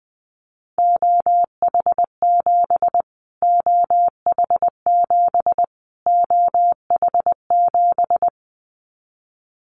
Morse code, heard as a keyed tone, about 20 wpm, 700 Hz. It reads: OH7 OH7 OH7
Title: Callsign Prefixes for Top 25 DXCC - Repeated 3x - 20wpm